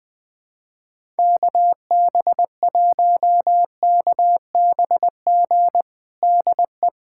Morse code, heard as a keyed tone, about 21 wpm, 700 Hz. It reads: KB1KBG DE
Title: USA Callsign DE Callsign 20wpm